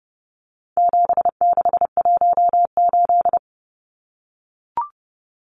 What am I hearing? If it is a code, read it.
Morse code, heard as a keyed tone, about 30 wpm, 700 Hz. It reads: 7618
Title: Numbers - 4-Digit 30wpm